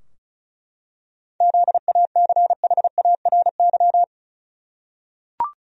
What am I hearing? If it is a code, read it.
Morse code, heard as a keyed tone, about 36 wpm, 700 Hz. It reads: ZACHARY
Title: USA Common Names - 35wpm